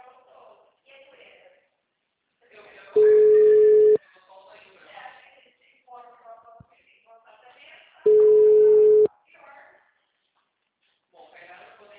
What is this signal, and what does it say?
Ringback tone (Europe, 425 Hz), with speech.
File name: out- -2036- - - .18702